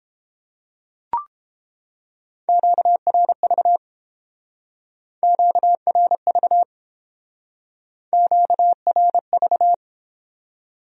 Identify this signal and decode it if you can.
Morse code, heard as a keyed tone, about 29 wpm, 700 Hz. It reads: QRV QRV QRV